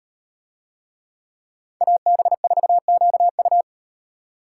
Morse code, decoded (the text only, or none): AB4QU